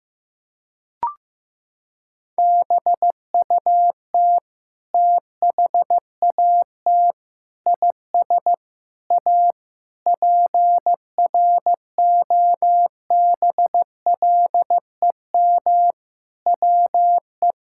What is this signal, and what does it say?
Morse code, heard as a keyed tone, about 16 wpm, 700 Hz. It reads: BUT THAT IS A PROBLEM WE
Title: Sentences from Top 1000 Words - Review Part 6 - 15wpm